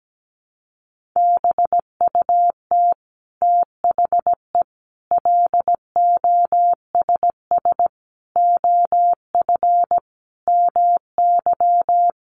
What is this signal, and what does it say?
Morse code, heard as a keyed tone, about 17 wpm, 700 Hz. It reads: BUT THE LOSS OF MY
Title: Sentences from Top 1400 Words 17wpm